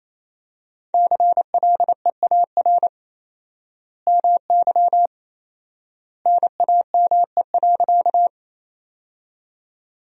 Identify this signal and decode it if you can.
Morse code, heard as a keyed tone, about 28 wpm, 700 Hz. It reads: CLEAR MY NAME.
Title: 4x Word-Spacing - Sentences from Top 500 Words - 28wpm